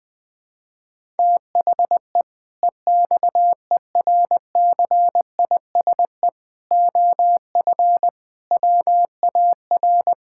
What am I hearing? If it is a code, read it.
Morse code, heard as a keyed tone, about 21 wpm, 700 Hz. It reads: THE EXERCISE OF WAR
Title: Sentences from Top 1000 Words - Review Part 6 - 20wpm